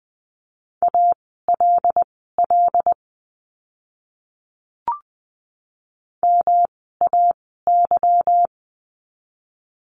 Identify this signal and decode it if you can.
Morse code, heard as a keyed tone, about 20 wpm, spaced as for 15 wpm, 700 Hz. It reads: ALL MAY
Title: 160 Lesson 26f - Letter g - Words 20wpm